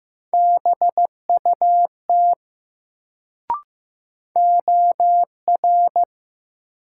Morse code, heard as a keyed tone, about 15 wpm, 700 Hz. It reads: BUT OR
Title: Sets of 2 Words - New Word Drill - 15wpm